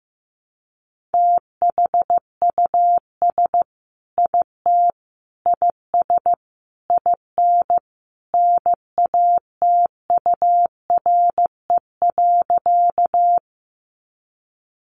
Morse code, heard as a keyed tone, about 15 wpm, 700 Hz. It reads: THUS IT IS IN NATURE.
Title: Sentences from Top 800 Words 15wpm